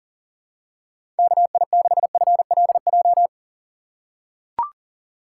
Morse code, heard as a keyed tone, about 40 wpm, 700 Hz. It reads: KI6FLJ
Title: USA Call Signs - 6x Character Spacing - 40wpm